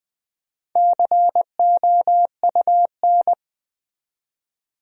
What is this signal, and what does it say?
Morse code, heard as a keyed tone, about 20 wpm, 700 Hz. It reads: COUN
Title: Build A Word - 1 Letter At A Time - 5-Letter Words - 20wpm